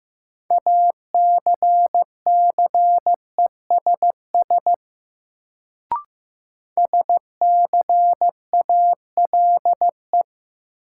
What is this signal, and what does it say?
Morse code, heard as a keyed tone, about 15 wpm, 700 Hz. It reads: ACCESS SCALE
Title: Sentences from Top 1000 Words - Review Part 4 - New Word Drill - 15wpm